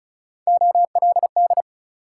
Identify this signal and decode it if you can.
Morse code, heard as a keyed tone, about 36 wpm, 700 Hz. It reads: OLD
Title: Sentences from Top 200 Words - New Word Drill - 35wpm